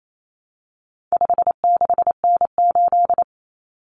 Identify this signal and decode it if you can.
Morse code, heard as a keyed tone, about 28 wpm, 700 Hz. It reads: 56N8
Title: Common Cut Numbers - 4-Digit 28wpm